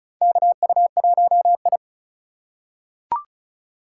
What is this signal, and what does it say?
Morse code, heard as a keyed tone, about 36 wpm, 700 Hz. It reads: KU1I
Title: USA Call Signs - 3x Character Spacing - 35wpm